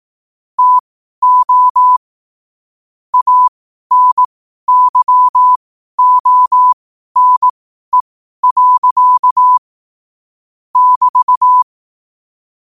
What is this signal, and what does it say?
Morse code, heard as a keyed tone, about 18 wpm, spaced as for 12 wpm, 1000 Hz. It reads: TO ANYONE. <BT>
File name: qotdfiles / 12wpm